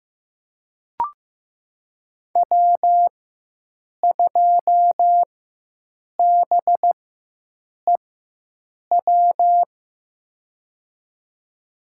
Morse code, heard as a keyed tone, about 15 wpm, spaced as for 7 wpm, 700 Hz. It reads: W2BEW
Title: USA Call Signs - 5 Characters - 4x Character Spacing - 15wpm